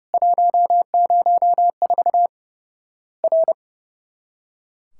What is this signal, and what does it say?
Morse code, heard as a keyed tone, about 32 wpm, 700 Hz. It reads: 104 R
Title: Worked All Europe DX Contest - QTC Exchange - 30wpm